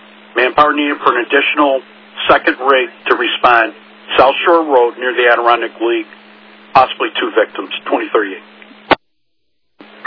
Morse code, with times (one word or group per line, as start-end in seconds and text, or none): none